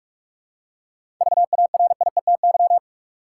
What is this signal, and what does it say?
Morse code, heard as a keyed tone, about 46 wpm, 700 Hz. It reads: VARIETY
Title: Seven Letter Words - 45wpm